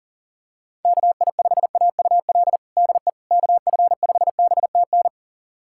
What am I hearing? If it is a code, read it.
Morse code, heard as a keyed tone, about 40 wpm, 700 Hz. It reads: KI5AUL DE KF5BTN
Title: USA Callsign DE Callsign - 6 Characters - 40wpm